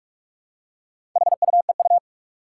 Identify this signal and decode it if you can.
Morse code, heard as a keyed tone, about 45 wpm, 700 Hz. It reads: HUEU
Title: Four Challenging Letter-Number Combinations 45wpm